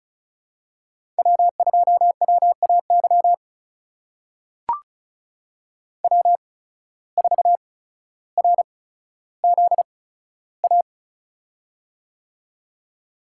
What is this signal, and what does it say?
Morse code, heard as a keyed tone, about 35 wpm, spaced as for 15 wpm, 700 Hz. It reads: W2WAY W4RZA
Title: USA Call Signs - 5 Characters - 8x Character Spacing - 35wpm